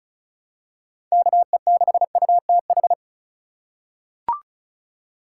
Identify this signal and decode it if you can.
Morse code, heard as a keyed tone, about 36 wpm, 700 Hz. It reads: KE6UTH
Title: USA Call Signs - 6 Characters - 3x Character Spacing - 35wpm